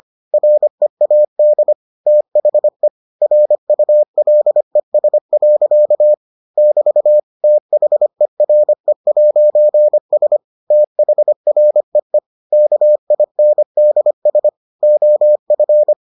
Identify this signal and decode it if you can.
Morse code, heard as a keyed tone, about 25 wpm, 600 Hz. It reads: READ THE RULES. <BT> THERE'S THREE KINDS OF